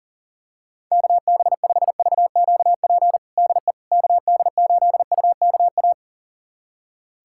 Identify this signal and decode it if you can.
Morse code, heard as a keyed tone, about 37 wpm, 700 Hz. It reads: KB5VQP DE KD8UKA
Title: USA Callsign DE Callsign - 6 Characters - 40wpm